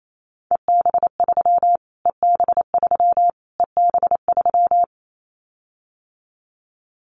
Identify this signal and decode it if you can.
Morse code, heard as a keyed tone, about 28 wpm, 700 Hz. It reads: EB3 EB3 EB3
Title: Callsign Prefixes for Top 25 DXCC - Repeated 3x - 28wpm